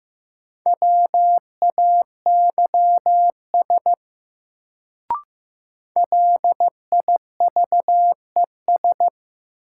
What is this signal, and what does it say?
Morse code, heard as a keyed tone, about 15 wpm, 700 Hz. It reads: WAYS LIVES
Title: Sets of 5 Words Encore - New Word Drill - 15wpm